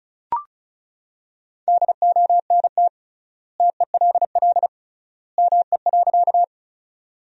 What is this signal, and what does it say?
Morse code, heard as a keyed tone, about 36 wpm, 700 Hz. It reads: DONT TELL ME.
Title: 3x Word-Spacing - Sentences from Top 300 Words - 35wpm